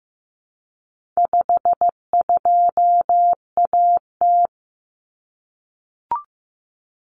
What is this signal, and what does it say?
Morse code, heard as a keyed tone, about 15 wpm, 700 Hz. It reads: 52AT
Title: Common Cut Numbers - 4-Digit 15wpm